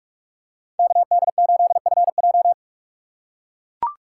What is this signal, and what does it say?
Morse code, heard as a keyed tone, about 42 wpm, 700 Hz. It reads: KD8FJ
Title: USA Call Signs - 5 Characters - 6x Character Spacing - 45wpm